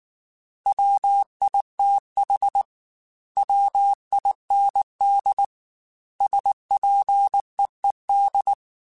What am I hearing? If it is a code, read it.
Morse code, heard as a keyed tone, about 19 wpm, 800 Hz. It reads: WITH WIND SPEED
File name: Weather Sets / Weather Sets-mp3s / Weather 5 12 @18